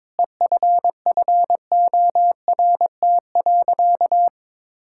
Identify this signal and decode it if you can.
Morse code, heard as a keyed tone, about 22 wpm, 700 Hz. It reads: EFFORT.
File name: practice / 22wpm / Sentences with 7-Word Phrases - 22wpm